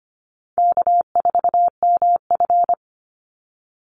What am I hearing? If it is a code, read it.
Morse code, heard as a keyed tone, about 25 wpm, 700 Hz. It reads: K4MF
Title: NA SOTA Chaser Callsigns 25wpm